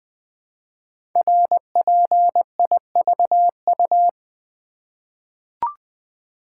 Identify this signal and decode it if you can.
Morse code, heard as a keyed tone, about 20 wpm, 700 Hz. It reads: RPIVU
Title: Code Groups - 5 Letters - 20wpm